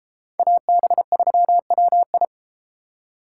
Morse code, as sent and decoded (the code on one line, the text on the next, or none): .- -... ...-- .-- ..
AB3WI